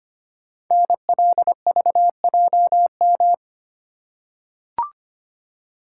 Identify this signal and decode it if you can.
Morse code, heard as a keyed tone, about 25 wpm, 700 Hz. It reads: NLVJM
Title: Code Groups - 5 Letters - 25wpm